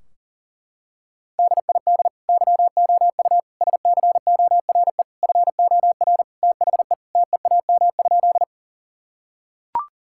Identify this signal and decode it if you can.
Morse code, heard as a keyed tone, about 37 wpm, 700 Hz. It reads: DID YOU SCORE FOR THE TEAM?
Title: Sentences from Top 1600 Words 40wpm